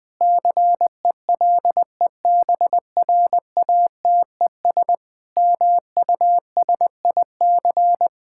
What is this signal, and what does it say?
Morse code, heard as a keyed tone, about 21 wpm, 700 Hz. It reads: CELEBRATES MUSIC
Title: Sentences with 7-Word Phrases - 20wpm